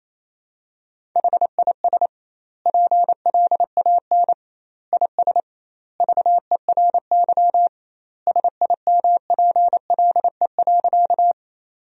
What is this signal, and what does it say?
Morse code, heard as a keyed tone, about 28 wpm, 700 Hz. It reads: HIS PLAN IS VERY SIMPLE.
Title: 2x Word-Spacing - Sentences from Top 500 Words - 28wpm